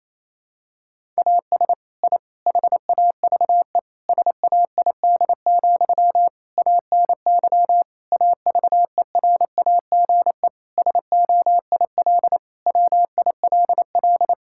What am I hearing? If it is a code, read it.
Morse code, heard as a keyed tone, about 28 wpm, 700 Hz. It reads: AS I HAVE SAID, ANY AVERAGE SOIL WILL
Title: Sentences from Top 1000 Words - Review Part 6 - 28wpm